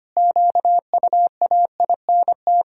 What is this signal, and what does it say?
Morse code, heard as a keyed tone, about 25 wpm, 700 Hz. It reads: QUAINT
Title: Pangram Challenge v2 - 25wpm